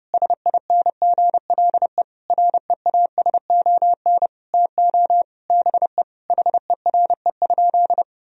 Morse code, heard as a keyed tone, about 30 wpm, 700 Hz. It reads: SINGLE REASON TO BE HERE?
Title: Sentences from Top 1000 Words - Review Part 4 - 30wpm